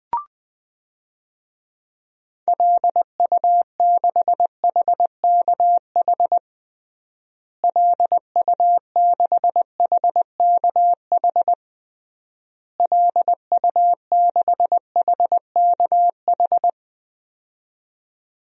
Morse code, heard as a keyed tone, about 21 wpm, 700 Hz. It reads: LU6HKH LU6HKH LU6HKH
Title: DXCC-100 Call Signs - 6 Characters - Repeated 3x - 20wpm